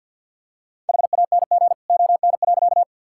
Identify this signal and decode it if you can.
Morse code, heard as a keyed tone, about 50 wpm, 700 Hz. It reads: HANG ON.
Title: Sentences from Top 1600 Words 50wpm